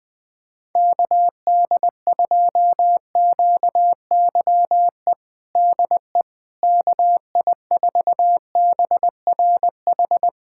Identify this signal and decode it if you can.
Morse code, heard as a keyed tone, about 20 wpm, 700 Hz. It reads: KD2QYE DE KI4BRH
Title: USA Callsign DE Callsign 20wpm